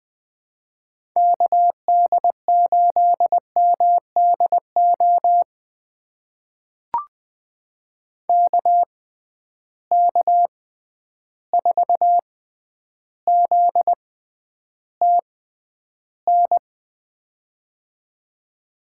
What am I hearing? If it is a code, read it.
Morse code, heard as a keyed tone, about 20 wpm, spaced as for 10 wpm, 700 Hz. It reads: KD8MDO KK4ZTN
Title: USA Call Signs - 6 Characters - 6x Character Spacing - 20wpm